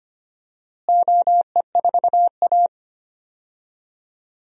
Morse code, heard as a keyed tone, about 25 wpm, 700 Hz. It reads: OE4A
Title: DXCC-100 Call Signs - 4 Characters - Repeated 2x - 25wpm